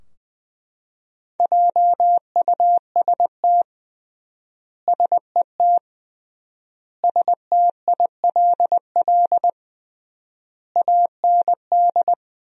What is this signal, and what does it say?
Morse code, heard as a keyed tone, about 20 wpm, 700 Hz. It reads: JUST SET STILL AND
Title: 3x Word-Spacing - Sentences from Top 300 Words - 20wpm